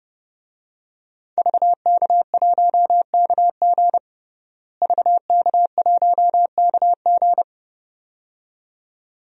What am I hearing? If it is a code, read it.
Morse code, heard as a keyed tone, about 30 wpm, 700 Hz. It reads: VK1KG VK1KG
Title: DXCC-100 Call Signs - 5 Characters - Repeated 2x - 30wpm